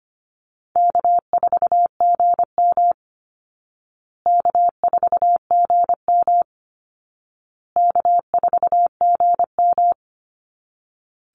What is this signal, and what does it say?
Morse code, heard as a keyed tone, about 25 wpm, 700 Hz. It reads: K4GM K4GM K4GM
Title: Easy Call Signs Repeated 3x - 25wpm